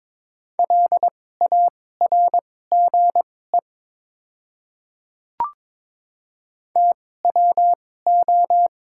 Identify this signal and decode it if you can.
Morse code, heard as a keyed tone, about 23 wpm, spaced as for 16 wpm, 700 Hz. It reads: LARGE TWO
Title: 2x Character Spacing - Top 500 Words - 22wpm